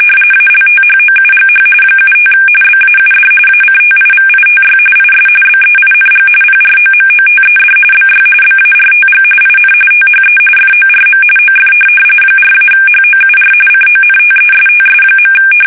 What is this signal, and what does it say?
RTTY, about 45.45 baud. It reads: ALTHOUGH 75 BAUD ACTIVITY IS BEING PROMOTED BY BARTG IN THE FORM OF 4-HOUR CONTESTS.
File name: fsk850 / teletype-RTTY